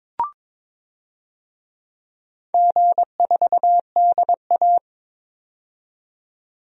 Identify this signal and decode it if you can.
Morse code, heard as a keyed tone, about 23 wpm, 700 Hz. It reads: G4DA
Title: DXCC-100 Call Signs - 4 Characters - 22wpm